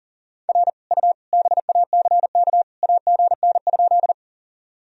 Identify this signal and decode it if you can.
Morse code, heard as a keyed tone, about 40 wpm, 700 Hz. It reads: R U BACK AGN?
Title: Sentences from Top 200 Words and Abbreviations 40wpm